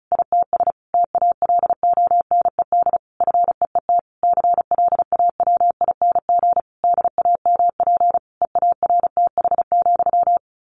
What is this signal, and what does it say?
Morse code, heard as a keyed tone, about 34 wpm, 700 Hz. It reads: ITS TALONED FEET CLAWING DAMP EARTH,
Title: A Sound Of Thunder 35wpm